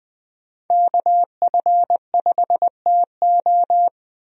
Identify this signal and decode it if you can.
Morse code, heard as a keyed tone, about 20 wpm, 700 Hz. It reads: KF5TO
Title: USA Call Signs - 5 Characters - 20wpm